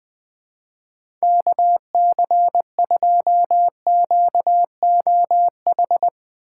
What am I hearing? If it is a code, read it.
Morse code, heard as a keyed tone, about 20 wpm, 700 Hz. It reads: KC2QOH
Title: Speed Racing - USA 2x3 Call Signs 20wpm